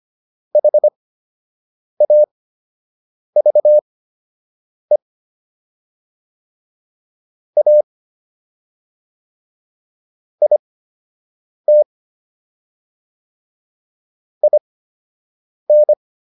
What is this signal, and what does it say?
Morse code, heard as a keyed tone, about 25 wpm, spaced as for 7 wpm, 600 Hz. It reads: HAVE A IT IN